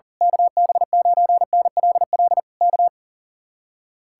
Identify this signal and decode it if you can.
Morse code, heard as a keyed tone, about 38 wpm, 700 Hz. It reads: KB9NLL K
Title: Speed-Racing Common QSO Phrases Encore - 40wpm